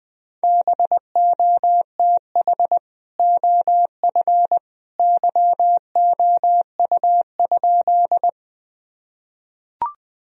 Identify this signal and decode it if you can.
Morse code, heard as a keyed tone, about 20 wpm, 700 Hz. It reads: BOTH OF YOU?
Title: Sentences from Top 1000 Words - Review Part 1 - 20wpm